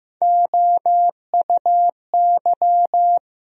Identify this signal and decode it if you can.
Morse code, heard as a keyed tone, about 16 wpm, 700 Hz. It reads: OUY
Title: USA Call Sign Postfixes - 3 Letters 15wpm